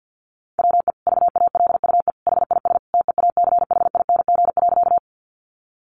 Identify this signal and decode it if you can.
Morse code, heard as a keyed tone, about 50 wpm, 700 Hz. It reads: WE VALUE HIS TEACHING.
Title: Sentences from Top 1100 Words 50wpm